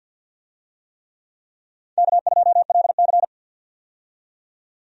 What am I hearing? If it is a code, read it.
Morse code, heard as a keyed tone, about 50 wpm, 700 Hz. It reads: K2LC
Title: USA Call Signs - 4 Characters - 50wpm